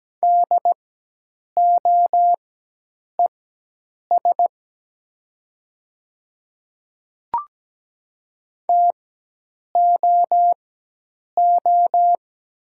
Morse code, heard as a keyed tone, about 17 wpm, spaced as for 8 wpm, 700 Hz. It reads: DOES TOO
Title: 4x Character Spacing - Top 500 Words - 17wpm